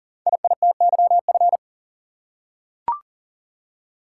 Morse code, heard as a keyed tone, about 38 wpm, 690 Hz. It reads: IITYF
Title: Code Groups - 5 Letters - 40wpm